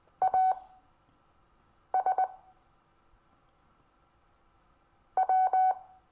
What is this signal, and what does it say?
Morse code, heard as a keyed tone, about 20 wpm, spaced as for 6 wpm, 750 Hz. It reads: AS W
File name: cw / Lesson22 REVIEW